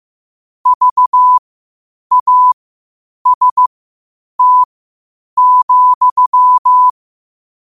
Morse code, heard as a keyed tone, about 15 wpm, spaced as for 8 wpm, 1000 Hz. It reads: VAST,
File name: qotdfiles / 8wpm